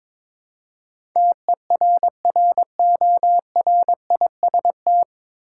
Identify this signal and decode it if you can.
Morse code, heard as a keyed tone, about 22 wpm, 700 Hz. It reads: TERRORIST
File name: 9 Letter Words - First 3 Letters Given - 22wpm